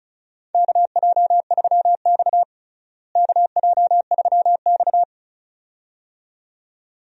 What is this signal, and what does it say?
Morse code, heard as a keyed tone, about 34 wpm, 700 Hz. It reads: KJ3X KJ3X
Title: USA Call Signs - 4 Characters - Repeated 2x - 35wpm